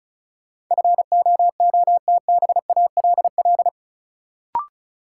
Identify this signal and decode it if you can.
Morse code, heard as a keyed tone, about 36 wpm, 700 Hz. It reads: FOOTBALL
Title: Sentences from Top 1600 Words - New Word Drill Alt - 35wpm